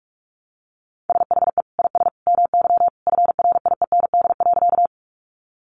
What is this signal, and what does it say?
Morse code, heard as a keyed tone, about 44 wpm, 700 Hz. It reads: SHE IS MY FRIEND.
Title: Sentences from Top 300 Words 45wpm